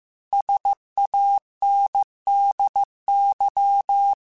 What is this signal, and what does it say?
Morse code, heard as a keyed tone, about 15 wpm, 800 Hz. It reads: SANDY